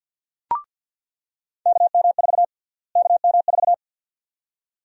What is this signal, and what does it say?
Morse code, heard as a keyed tone, about 47 wpm, 700 Hz. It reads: KM4 KM4
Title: USA Call Sign Prefixes - Repeated 2x - 50wpm